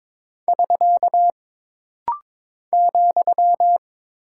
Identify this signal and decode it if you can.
Morse code, heard as a keyed tone, about 22 wpm, 700 Hz. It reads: <SK> ,
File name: practice / 22wpm / Single Punctuation-Prosign - Rapid-Fire - 22wpm